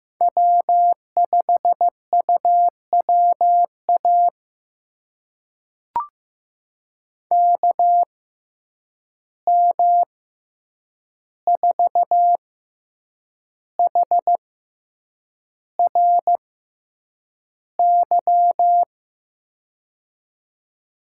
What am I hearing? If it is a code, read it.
Morse code, heard as a keyed tone, about 15 wpm, spaced as for 7 wpm, 700 Hz. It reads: W5UWA KM4HRY
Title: USA Call Signs - 6x Character Spacing - 15wpm